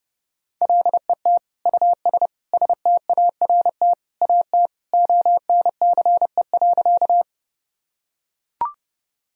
Morse code, heard as a keyed tone, about 30 wpm, 700 Hz. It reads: LET US START AT ONCE.
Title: Sentences from Top 1000 Words - Review Part 1 - 30wpm